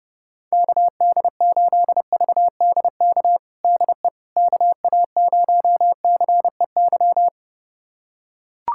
Morse code, heard as a keyed tone, about 30 wpm, 700 Hz. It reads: KD8VDK DE KA0CEY
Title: USA Callsign DE Callsign 30wpm